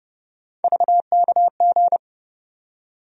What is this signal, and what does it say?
Morse code, heard as a keyed tone, about 31 wpm, 700 Hz. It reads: VKG
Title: USA Call Sign Postfixes - 3 Letters 30wpm